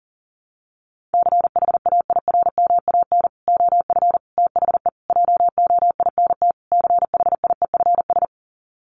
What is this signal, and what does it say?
Morse code, heard as a keyed tone, about 40 wpm, 700 Hz. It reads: CHAIRMAN OF THE JOINT CHIEFS
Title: Sets of 5 Words Encore 40wpm